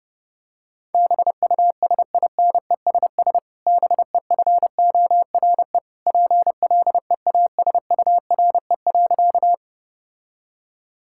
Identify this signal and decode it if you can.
Morse code, heard as a keyed tone, about 30 wpm, 700 Hz. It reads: BUSINESS BEFORE PLEASURE.
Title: Sentences from Top 1600 Words 30wpm